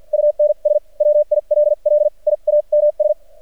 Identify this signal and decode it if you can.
Morse code, heard as a keyed tone, about 48 wpm, spaced as for 38 wpm, 600 Hz. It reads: UNS WILLIAMS
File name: archived / CRSnet